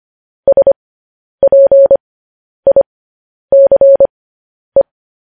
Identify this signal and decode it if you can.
Morse code, heard as a keyed tone, about 25 wpm, spaced as for 10 wpm, 550 Hz. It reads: SPICE
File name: podcast / News headlines-1x-25x10